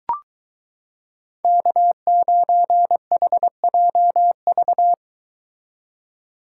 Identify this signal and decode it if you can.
Morse code, heard as a keyed tone, about 23 wpm, 700 Hz. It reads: K9HJV